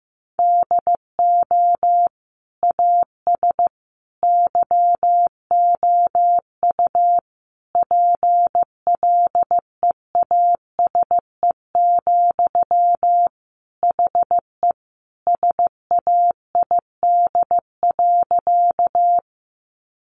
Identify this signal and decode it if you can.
Morse code, heard as a keyed tone, about 15 wpm, 700 Hz. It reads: DO AS YOU PLEASE, HE SAID.
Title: Sentences from Top 1500 Words 15wpm